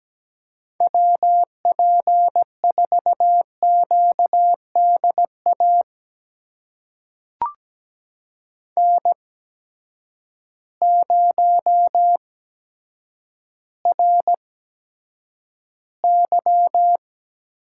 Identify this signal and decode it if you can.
Morse code, heard as a keyed tone, about 17 wpm, spaced as for 7 wpm, 700 Hz. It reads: WP4QD AN0RY
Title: USA Call Signs - 8x Character Spacing - 17wpm